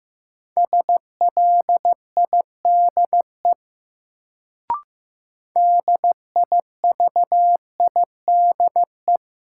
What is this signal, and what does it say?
Morse code, heard as a keyed tone, about 15 wpm, 700 Hz. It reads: SLIDE DIVIDE
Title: Sentences from Top 1600 Words - New Word Drill Alt - 15wpm